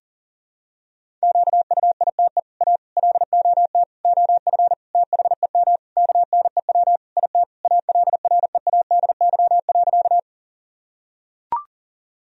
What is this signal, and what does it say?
Morse code, heard as a keyed tone, about 40 wpm, 700 Hz. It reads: QUITE A LOT OF THEM KNEW IT ALREADY.
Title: Sentences from Top 1000 Words - Review Part 3 - 40wpm